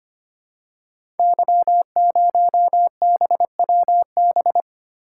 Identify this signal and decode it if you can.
Morse code, heard as a keyed tone, about 25 wpm, 700 Hz. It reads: Y0BWB
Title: Code Groups - 5 Letter:Numbers - 25wpm